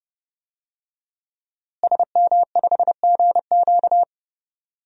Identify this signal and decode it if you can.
Morse code, heard as a keyed tone, about 30 wpm, 700 Hz. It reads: SM5GQ
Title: DXCC-100 Call Signs - 5 Characters - Repeated 3x - 30wpm